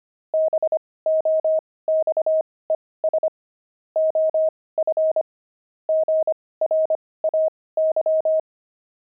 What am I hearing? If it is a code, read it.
Morse code, heard as a keyed tone, about 25 wpm, spaced as for 18 wpm, 630 Hz. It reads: BOXES OF GRAY
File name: pangram-fs-18-cs-25-nine